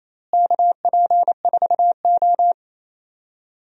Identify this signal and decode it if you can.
Morse code, heard as a keyed tone, about 28 wpm, 700 Hz. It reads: KP4O
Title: DXCC-100 Call Signs - 4 Characters - Repeated 3x - 28wpm